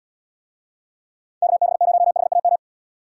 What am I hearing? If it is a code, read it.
Morse code, heard as a keyed tone, about 75 wpm, 700 Hz. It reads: KB0HAM